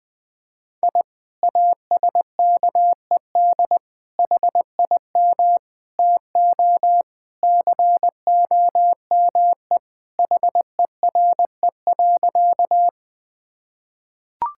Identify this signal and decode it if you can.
Morse code, heard as a keyed tone, about 20 wpm, 700 Hz. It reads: I ASKED HIM TO COME HERE.
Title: Sentences with 2-Word Phrases - 20wpm